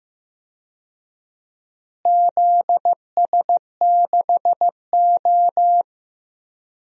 Morse code, heard as a keyed tone, about 15 wpm, 700 Hz. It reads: ZS6O
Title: DXCC-100 Call Signs - 4 Characters - Repeated 3x - 15wpm